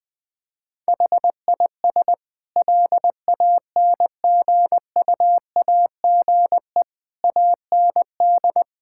Morse code, heard as a keyed tone, about 20 wpm, 700 Hz. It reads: HIS LANGUAGE AND
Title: Sentences from Top 1600 Words 20wpm